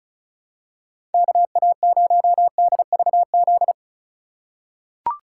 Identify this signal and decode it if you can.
Morse code, heard as a keyed tone, about 36 wpm, 700 Hz. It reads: KA0DVZ
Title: USA Call Signs - 6 Characters - 3x Character Spacing - 35wpm